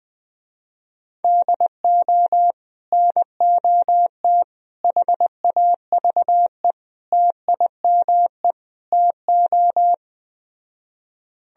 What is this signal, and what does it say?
Morse code, heard as a keyed tone, about 20 wpm, 700 Hz. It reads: DO NOT HAVE TIME TO
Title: Speed-Racing - Sets of 5 Words 17wpm